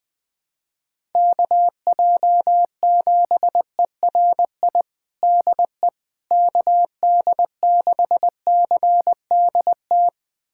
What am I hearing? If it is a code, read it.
Morse code, heard as a keyed tone, about 20 wpm, 700 Hz. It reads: KJ7ERI DE KD6CDT
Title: USA Callsign DE Callsign 20wpm